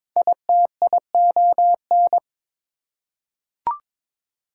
Morse code, heard as a keyed tone, about 22 wpm, 700 Hz. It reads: ITION
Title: Prefixes and Suffixes 22wpm